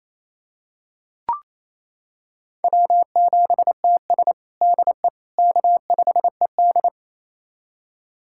Morse code, heard as a keyed tone, about 28 wpm, 700 Hz. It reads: W7TS DE K5ED
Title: USA Callsign DE Callsign - 4 Characters - 28wpm